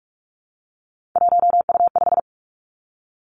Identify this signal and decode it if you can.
Morse code, heard as a keyed tone, about 46 wpm, 700 Hz. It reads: 1U5
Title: Three Challenging Letter-Number Combinations 45wpm